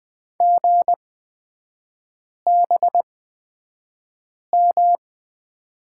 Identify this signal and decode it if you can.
Morse code, heard as a keyed tone, about 20 wpm, spaced as for 5 wpm, 700 Hz. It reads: GBM
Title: 243 Lesson 39e - Letter z - Three Characters 20wpm